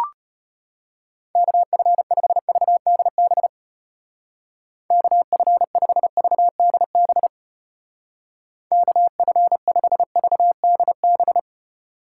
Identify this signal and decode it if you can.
Morse code, heard as a keyed tone, about 31 wpm, 700 Hz. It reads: KF5VDB KF5VDB KF5VDB